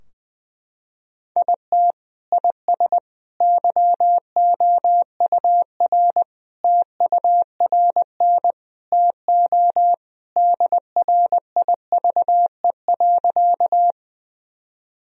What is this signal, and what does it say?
Morse code, heard as a keyed tone, about 20 wpm, 700 Hz. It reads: IT IS YOUR TURN TO DRIVE.
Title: Sentences from Top 1000 Words - Review Part 2 - 20wpm